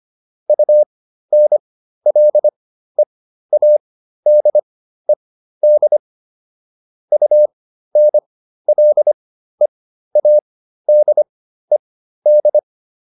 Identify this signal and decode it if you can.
Morse code, heard as a keyed tone, about 25 wpm, spaced as for 13 wpm, 600 Hz. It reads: UNLEADED UNLEADED